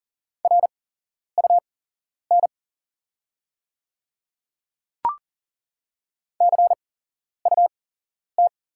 Morse code, heard as a keyed tone, about 40 wpm, spaced as for 11 wpm, 690 Hz. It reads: RUN CUT
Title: 8x Character Spacing - Top 500 Words - 40wpm